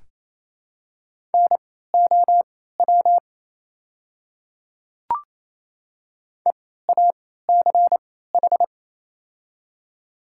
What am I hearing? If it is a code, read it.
Morse code, heard as a keyed tone, about 28 wpm, spaced as for 16 wpm, 700 Hz. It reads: NOW EACH
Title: 3x Character Spacing - Top 100 Words - 28wpm